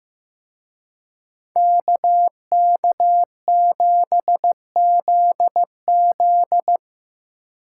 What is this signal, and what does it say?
Morse code, heard as a keyed tone, about 18 wpm, 700 Hz. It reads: KK7ZZ
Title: Speed Racing - USA 2x2 Call Signs 15wpm